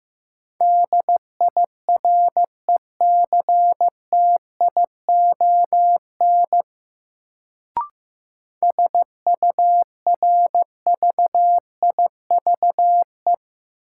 Morse code, heard as a keyed tone, about 15 wpm, 700 Hz. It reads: DIRECTION SURVIVE
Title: Sentences from Top 1300 Words - New Word Drill Alt - 15wpm